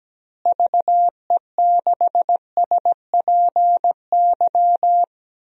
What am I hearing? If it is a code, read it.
Morse code, heard as a keyed tone, about 17 wpm, 700 Hz. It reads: VE6SPY
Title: DXCC-100 Call Signs - 6 Characters - Repeated 2x - 17wpm